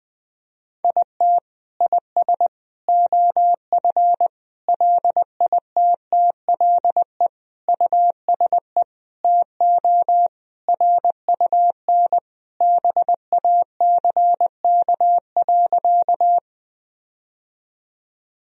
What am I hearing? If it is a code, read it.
Morse code, heard as a keyed tone, about 20 wpm, 700 Hz. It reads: IT IS OF LITTLE USE TO RUN BACK.
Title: Sentences from Top 1000 Words - Review Part 1 - 20wpm